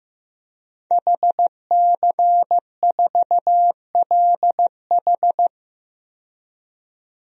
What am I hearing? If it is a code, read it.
Morse code, heard as a keyed tone, about 15 wpm, 700 Hz. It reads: HC4LH
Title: DXCC-100 Call Signs - 5 Characters - 15wpm